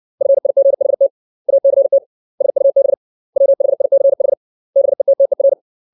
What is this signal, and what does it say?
Morse code, heard as a keyed tone, about 60 wpm, 550 Hz. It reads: FIGHT ACT SUB WHICH BETTER
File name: 632 words / 632 Words 60WPM